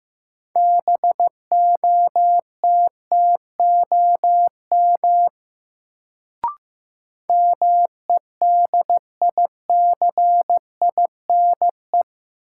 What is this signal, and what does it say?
Morse code, heard as a keyed tone, about 15 wpm, 700 Hz. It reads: BOTTOM MEDICINE
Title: Sentences from Top 900 Words - New Word Drill - 15wpm